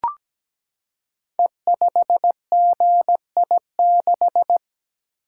Morse code, heard as a keyed tone, about 17 wpm, 700 Hz. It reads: E5GI6
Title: Code Groups - 5 Letter:Numbers - 17wpm